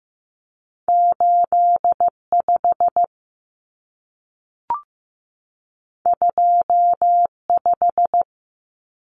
Morse code, heard as a keyed tone, about 15 wpm, 700 Hz. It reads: 85 25
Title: Common Cut Numbers - 2-Digit 15wpm